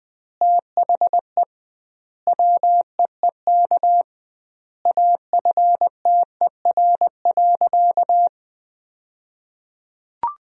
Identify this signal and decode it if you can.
Morse code, heard as a keyed tone, about 20 wpm, 700 Hz. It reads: THE WEEK AFTER.
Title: 2x Word-Spacing - Sentences from Top 500 Words - 20wpm